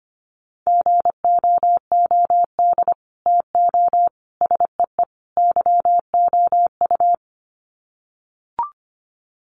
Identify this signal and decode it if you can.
Morse code, heard as a keyed tone, about 25 wpm, 700 Hz. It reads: GOOD TO SEE YOU
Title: Sets of 4 Words 25wpm